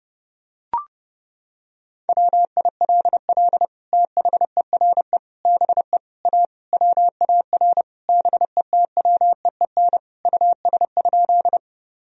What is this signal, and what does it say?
Morse code, heard as a keyed tone, about 30 wpm, 700 Hz. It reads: WILL THERE BE A WAR BETWEEN US?
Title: Sentences from Top 1000 Words - Review Part 2 - 30wpm